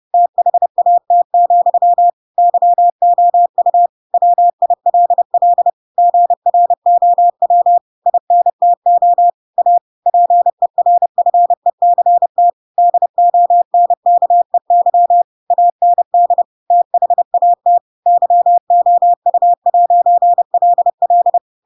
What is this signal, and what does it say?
Morse code, heard as a keyed tone, about 30 wpm, 700 Hz. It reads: THAT, YOU WILL GROW INTO A PERFECT DONKEY AND THAT YOU'LL
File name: Pinocchio / Pinocchio ch4 30 30 ik1hge